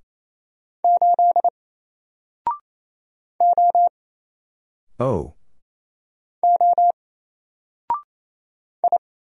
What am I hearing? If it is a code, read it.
Morse code, heard as a keyed tone, about 27 wpm, 700 Hz. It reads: 8 O O I
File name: Single-Letter-Number-28wpm